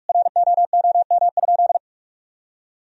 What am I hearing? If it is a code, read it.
Morse code, heard as a keyed tone, about 41 wpm, 700 Hz. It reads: ROOM?
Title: Sentences from Top 1000 Words - Review Part 4 - 45wpm